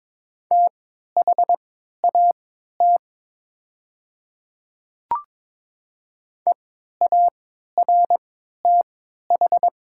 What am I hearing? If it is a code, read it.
Morse code, heard as a keyed tone, about 22 wpm, spaced as for 12 wpm, 700 Hz. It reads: THAT EARTH
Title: 3x Character Spacing - Top 500 Words - 22wpm